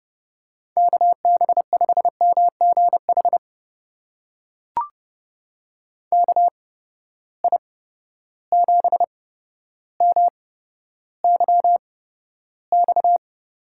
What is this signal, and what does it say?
Morse code, heard as a keyed tone, about 30 wpm, spaced as for 13 wpm, 700 Hz. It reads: KB5MGH KI7MYX
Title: USA Call Signs - 6 Characters - 8x Character Spacing - 30wpm